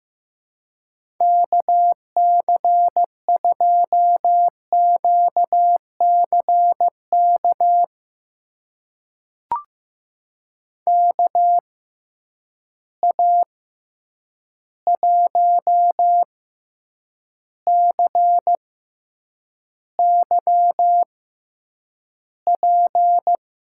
Morse code, heard as a keyed tone, about 15 wpm, spaced as for 8 wpm, 700 Hz. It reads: KC2QCK KA1CYP
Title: USA Call Signs - 6 Characters - 6x Character Spacing - 15wpm